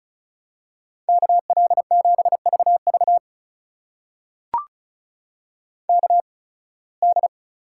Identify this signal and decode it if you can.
Morse code, heard as a keyed tone, about 36 wpm, spaced as for 23 wpm, 700 Hz. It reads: KL7VV KD
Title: USA Call Signs - 8x Character Spacing - 35wpm